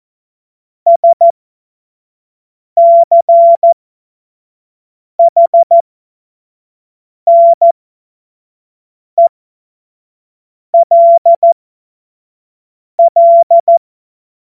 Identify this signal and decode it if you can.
Morse code, heard as a keyed tone, about 14 wpm, spaced as for 5 wpm, 680 Hz. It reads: SCHNELL
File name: Morsekurs / Witze / Witz 09. Dezember 2025 25 F70